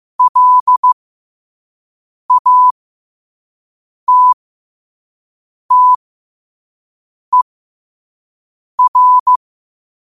Morse code, heard as a keyed tone, about 15 wpm, spaced as for 5 wpm, 1000 Hz. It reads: LATTER
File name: qotdfiles / 5wpm